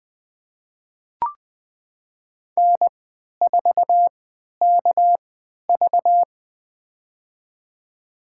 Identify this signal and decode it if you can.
Morse code, heard as a keyed tone, about 20 wpm, spaced as for 11 wpm, 700 Hz. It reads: N4KV
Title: USA Call Signs - 4 Characters - 3x Character Spacing - 20wpm